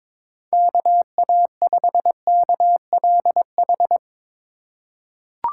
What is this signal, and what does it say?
Morse code, heard as a keyed tone, about 22 wpm, 700 Hz. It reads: KA5KLH
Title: USA Call Signs - 6 Characters - 2x Character Spacing - 22wpm